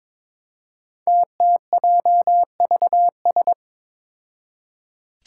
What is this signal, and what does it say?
Morse code, heard as a keyed tone, about 22 wpm, 700 Hz. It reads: TTJVS
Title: Five Challenging Letter-Number Combinations 22wpm